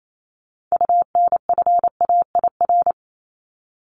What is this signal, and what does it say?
Morse code, heard as a keyed tone, about 28 wpm, 700 Hz. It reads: UNFAIR
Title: Top 1751-2000 Words 28wpm